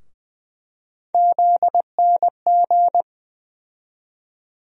Morse code, heard as a keyed tone, about 21 wpm, 700 Hz. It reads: ZNG
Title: USA Call Sign Postfixes - 3 Letters 20wpm